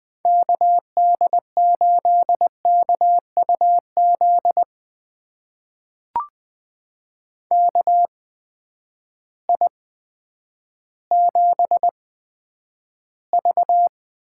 Morse code, heard as a keyed tone, about 20 wpm, spaced as for 10 wpm, 700 Hz. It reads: KD8KUZ KI7V
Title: USA Call Signs - 6 Characters - 8x Character Spacing - 20wpm